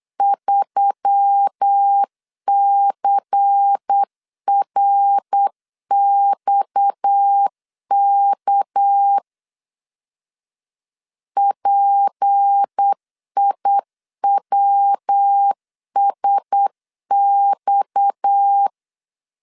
Morse code, heard as a keyed tone, about 8 wpm, 800 Hz. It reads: 3CRXK PIWSX
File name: LEZIONI / 6 PKX0 / 6 PKX 8wpm 3041